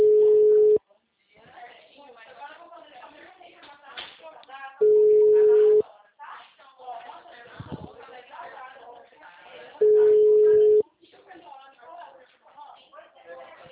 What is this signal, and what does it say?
Ringback tone (Europe, 425 Hz), with speech.